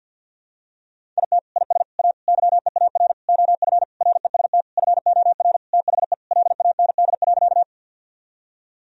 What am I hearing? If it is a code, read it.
Morse code, heard as a keyed tone, about 47 wpm, 700 Hz. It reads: IT IS A YEAR OF REST FOR THE LAND.
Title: Sentences from Top 1000 Words - Review Part 1 - 50wpm